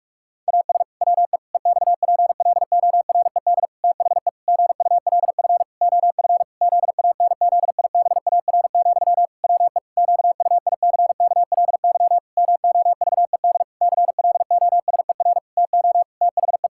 Morse code, heard as a keyed tone, about 41 wpm, 700 Hz. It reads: AS WE EXPLORED THE GULF OF ZANZIBAR, WE QUICKLY MOVED CLOSER TO THE
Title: Pangram Challenge v2 - 45wpm